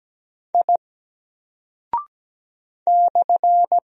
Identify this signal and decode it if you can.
Morse code, heard as a keyed tone, about 17 wpm, 700 Hz. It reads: I /
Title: Single Character (Letter:Number:Punctuation:Prosign) Alt 17wpm